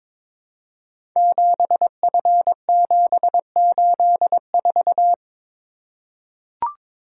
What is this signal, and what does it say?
Morse code, heard as a keyed tone, about 22 wpm, 700 Hz. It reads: 7F784
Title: Code Groups - 5 Letter:Numbers - 22wpm